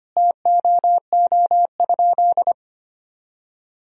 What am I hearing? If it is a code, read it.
Morse code, heard as a keyed tone, about 25 wpm, 700 Hz. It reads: TOO?
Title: Sentences from Top 1000 Words - Review Part 4 - 25wpm